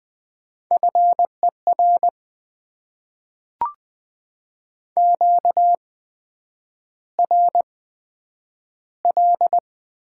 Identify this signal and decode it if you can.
Morse code, heard as a keyed tone, about 20 wpm, spaced as for 9 wpm, 700 Hz. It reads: FER QRL
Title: Top 20 QSO Elements - 8x Character Spacing - 20wpm